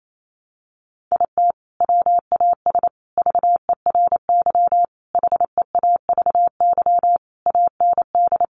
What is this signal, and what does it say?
Morse code, heard as a keyed tone, about 28 wpm, 700 Hz. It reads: IT WAS VERY HEAVY AND
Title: Sentences from Top 1100 Words 28wpm